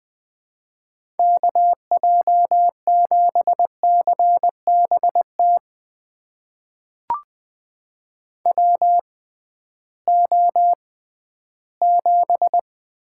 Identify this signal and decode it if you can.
Morse code, heard as a keyed tone, about 20 wpm, spaced as for 13 wpm, 700 Hz. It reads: KJ7CBT WO7
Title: USA Call Signs - 6x Character Spacing - 20wpm